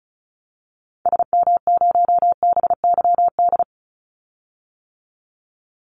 Morse code, heard as a keyed tone, about 36 wpm, 700 Hz. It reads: SM0BYD
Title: Call Signs - 35wpm